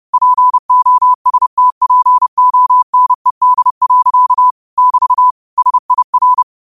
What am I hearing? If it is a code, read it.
Morse code, heard as a keyed tone, about 30 wpm, 1000 Hz. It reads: POSTPONED. <BT> SIR